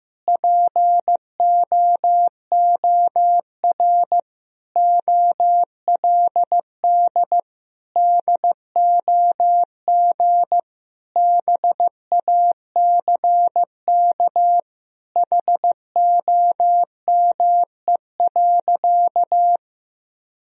Morse code, heard as a keyed tone, about 15 wpm, 700 Hz. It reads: POOR OLD DOG BACK HOME.
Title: Sentences from Top 1000 Words - Review Part 2 - 15wpm